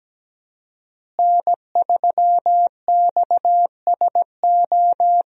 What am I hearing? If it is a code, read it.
Morse code, heard as a keyed tone, about 17 wpm, 700 Hz. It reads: N3XSO
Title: USA Call Signs - 5 Characters - Repeated 3x - 17wpm